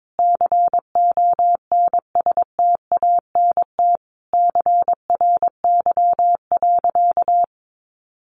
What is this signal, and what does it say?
Morse code, heard as a keyed tone, about 22 wpm, 700 Hz. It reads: CONSTANT CRY.
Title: Sentences from Top 1500 Words 22wpm